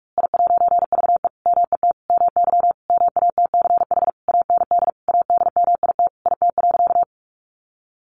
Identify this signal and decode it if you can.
Morse code, heard as a keyed tone, about 46 wpm, 700 Hz. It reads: I'VE MET MY MATCH AND ADMIT IT.
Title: A Sound Of Thunder 45wpm